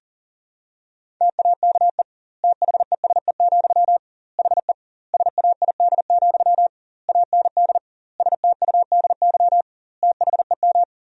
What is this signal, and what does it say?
Morse code, heard as a keyed tone, about 40 wpm, 700 Hz. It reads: TAKE THESE, HE SAID, AND STUDY THEM
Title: 2x Word-Spacing - Sentences from Top 300 Words - 40wpm